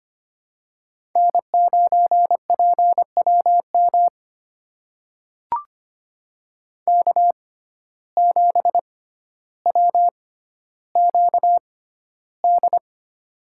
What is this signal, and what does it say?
Morse code, heard as a keyed tone, about 25 wpm, spaced as for 13 wpm, 700 Hz. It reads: N9PWM K7WQD
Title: USA Call Signs - 5 Characters - 6x Character Spacing - 25wpm